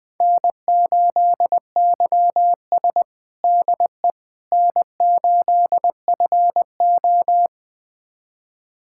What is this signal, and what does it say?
Morse code, heard as a keyed tone, about 20 wpm, 700 Hz. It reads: N8YS DE N8FO
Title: USA Callsign DE Callsign - 4 Characters - 20wpm